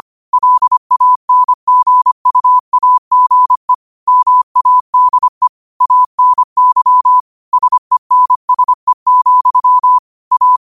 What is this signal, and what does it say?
Morse code, heard as a keyed tone, about 25 wpm, 1000 Hz. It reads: LANGUAGE MADE ANY SENSE, A
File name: qotdfiles / 25wpm